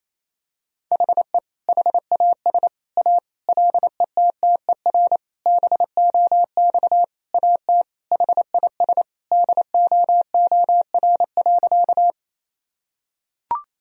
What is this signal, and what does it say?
Morse code, heard as a keyed tone, about 28 wpm, 700 Hz. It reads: HE HAS A LETTER BOX AT HIS DOOR.
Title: Sentences from Top 1000 Words - Review Part 2 - 28wpm